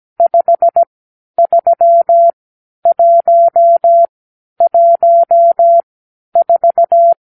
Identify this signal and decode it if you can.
Morse code, heard as a keyed tone, about 17 wpm, spaced as for 10 wpm, 680 Hz. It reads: 53114
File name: arrl / Review (1-5)